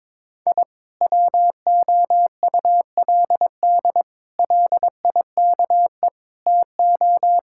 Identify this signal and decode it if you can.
Morse code, heard as a keyed tone, about 22 wpm, 700 Hz. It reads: I WOULD LIKE TO
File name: practice / 22wpm / Sentences with 4-Word Phrases - 22wpm